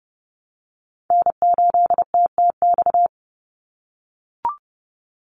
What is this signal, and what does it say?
Morse code, heard as a keyed tone, about 30 wpm, 700 Hz. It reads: N8TTX
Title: USA Call Signs 30wpm